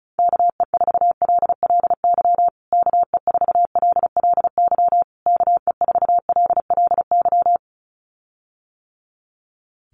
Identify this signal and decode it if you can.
Morse code, heard as a keyed tone, about 34 wpm, 700 Hz. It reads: KE4LLY KE4LLY KE4LLY
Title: USA Call Signs Repeated 3x 35wpm